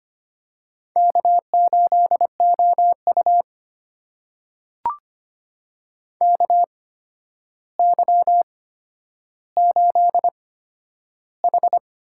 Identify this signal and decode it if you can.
Morse code, heard as a keyed tone, about 25 wpm, spaced as for 11 wpm, 700 Hz. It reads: K8OU KY8H
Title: USA Call Signs - 4 Characters - 8x Character Spacing - 25wpm